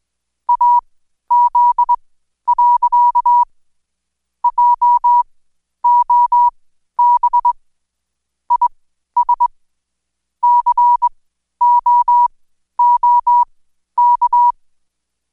Morse code, heard as a keyed tone, about 21 wpm, spaced as for 12 wpm, 970 Hz. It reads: AZ. JOB IS COOK
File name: Code Quick / AudioDownld2 / Seg121-Day26-DayDrill 26